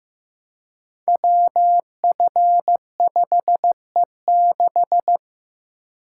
Morse code, heard as a keyed tone, about 15 wpm, 700 Hz. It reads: WF5E6
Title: Code Groups - 5 Letter:Numbers - 15wpm